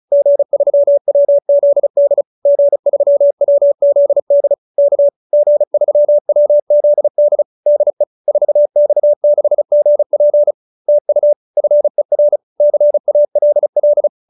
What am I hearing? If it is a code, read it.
Morse code, heard as a keyed tone, about 35 wpm, 590 Hz. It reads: G3WZD G3WZD K G3WZD DE 4X6GP TU FER CALL
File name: QSO310 35